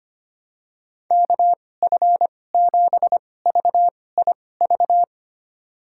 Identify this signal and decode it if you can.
Morse code, heard as a keyed tone, about 25 wpm, spaced as for 18 wpm, 700 Hz. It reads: KF7VIV
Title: USA Call Signs - 6 Characters - 2x Character Spacing - 25wpm